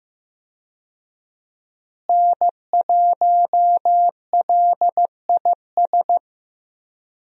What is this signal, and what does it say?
Morse code, heard as a keyed tone, about 15 wpm, 700 Hz. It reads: N1LIS
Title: USA Call Signs - 5 Characters - Repeated 3x - 15wpm